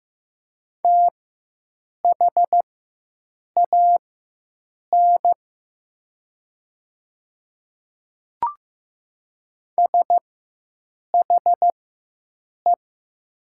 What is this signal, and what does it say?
Morse code, heard as a keyed tone, about 15 wpm, spaced as for 7 wpm, 700 Hz. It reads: THAN SHE
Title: 4x Character Spacing - Top 100 Words - 15wpm